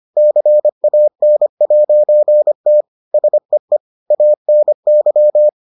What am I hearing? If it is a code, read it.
Morse code, heard as a keyed tone, about 25 wpm, 600 Hz. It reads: CAN'T SEE ANY